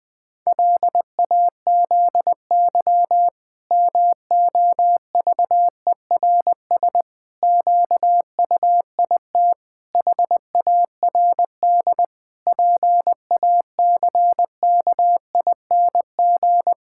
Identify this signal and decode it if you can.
Morse code, heard as a keyed tone, about 20 wpm, 700 Hz. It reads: LAZY MOVERS QUIT HARD PACKING
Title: Pangram Challenge v2 - 20wpm